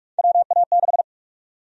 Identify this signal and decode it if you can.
Morse code, heard as a keyed tone, about 46 wpm, 700 Hz. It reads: WA6
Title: USA Call Sign Prefixes - 45wpm